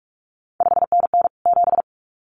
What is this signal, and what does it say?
Morse code, heard as a keyed tone, about 46 wpm, 700 Hz. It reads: 5NN 7
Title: CQ WW DX Contest Exchange - 45wpm